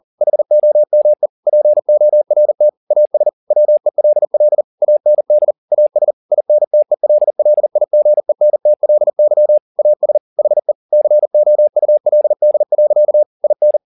Morse code, heard as a keyed tone, about 40 wpm, 600 Hz. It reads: HOME PORT AS WELL AND AS INTELLIGENTLY AS HE COULD. IN
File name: the old man and the sea 2 40wpm